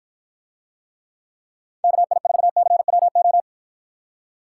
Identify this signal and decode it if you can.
Morse code, heard as a keyed tone, about 53 wpm, 700 Hz. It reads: KI4CWO